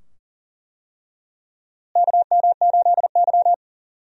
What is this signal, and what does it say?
Morse code, heard as a keyed tone, about 37 wpm, 700 Hz. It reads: KM8Y
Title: USA Call Signs - 4 Characters - Repeated 2x - 40wpm